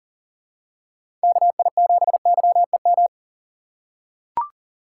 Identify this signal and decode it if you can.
Morse code, heard as a keyed tone, about 40 wpm, 700 Hz. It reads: KI7YEM
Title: USA Call Signs - 6 Characters - 2x Character Spacing - 40wpm